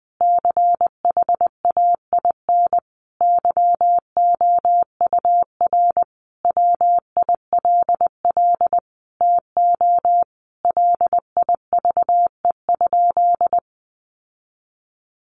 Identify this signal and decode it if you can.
Morse code, heard as a keyed tone, about 20 wpm, 700 Hz. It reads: CHAIN YOUR WILL TO LIVE?
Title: Sentences from Top 1500 Words 20wpm